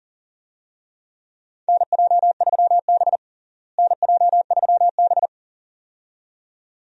Morse code, heard as a keyed tone, about 40 wpm, 700 Hz. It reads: NJ3B NJ3B
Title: USA Call Signs - 4 Characters - Repeated 2x - 40wpm